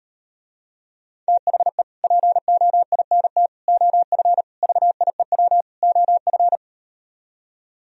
Morse code, heard as a keyed tone, about 38 wpm, 700 Hz. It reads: THE POINT OF VIEW OF